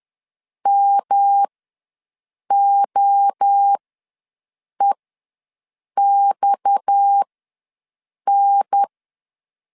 Morse code, heard as a keyed tone, about 10 wpm, spaced as for 6 wpm, 800 Hz. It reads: MOEXN
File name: LEZIONI / 8 YJ45 / 8 YJ45 ws 10wpm 3843